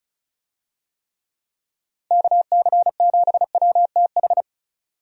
Morse code, heard as a keyed tone, about 36 wpm, 700 Hz. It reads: KC7WTH
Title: USA Call Signs - 6 Characters - 35wpm